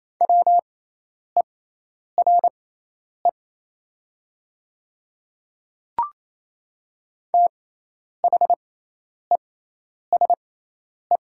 Morse code, heard as a keyed tone, about 27 wpm, spaced as for 10 wpm, 700 Hz. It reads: WERE THESE
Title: 6x Character Spacing - Top 100 Words - 28wpm